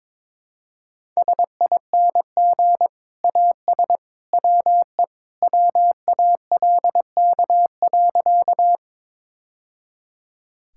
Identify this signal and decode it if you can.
Morse code, heard as a keyed tone, about 21 wpm, 700 Hz. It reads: SING AS WE WALK.
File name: practice / 22wpm / Sentences with 2-Word Phrases - 22wpm